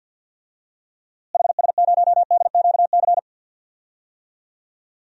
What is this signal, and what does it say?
Morse code, heard as a keyed tone, about 51 wpm, 700 Hz. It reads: HS0DQC
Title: DXCC-100 Call Signs - 6 Characters - Repeated 2x - 50wpm